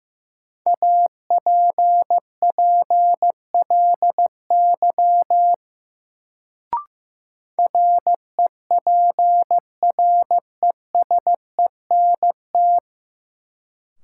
Morse code, heard as a keyed tone, about 15 wpm, 700 Hz. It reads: APPLY REPRESENT
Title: Sentences from Top 1200 Words - New Word Drill - 15wpm